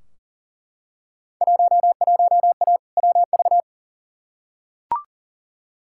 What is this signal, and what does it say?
Morse code, heard as a keyed tone, about 39 wpm, 700 Hz. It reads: 11A WV
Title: Field Day Exchanges Encore - 40wpm